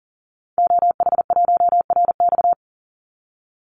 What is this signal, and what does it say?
Morse code, heard as a keyed tone, about 40 wpm, 700 Hz. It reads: OH1RX
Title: Call Signs - 40wpm